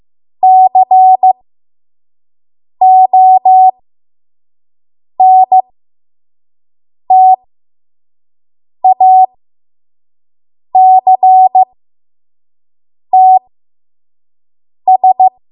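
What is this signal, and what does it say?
Morse code, heard as a keyed tone, about 15 wpm, spaced as for 5 wpm, 750 Hz. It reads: CONTACTS
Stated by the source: 5 WPM Morse code practice.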